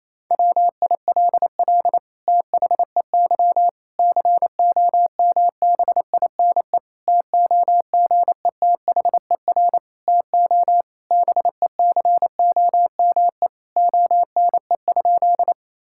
Morse code, heard as a keyed tone, about 28 wpm, 700 Hz. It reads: WILL THEY COMBINE TOGETHER TO BECOME ONE?
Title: Sentences from Top 1600 Words 28wpm